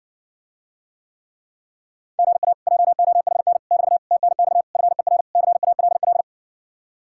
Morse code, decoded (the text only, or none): GA JOHN <BT> TNX FER CALL